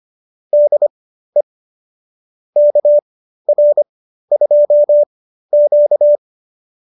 Morse code, heard as a keyed tone, about 25 wpm, spaced as for 13 wpm, 590 Hz. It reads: DE KR2Q